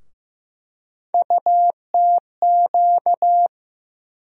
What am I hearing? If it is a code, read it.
Morse code, heard as a keyed tone, about 15 wpm, 700 Hz. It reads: UTQ
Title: USA Call Sign Postfixes - 3 Letters 15wpm